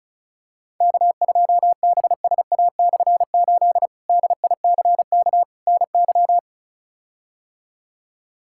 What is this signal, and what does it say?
Morse code, heard as a keyed tone, about 34 wpm, 700 Hz. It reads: K2BSA/8 DICK NY
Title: K1USN SST QSO Practice - 35wpm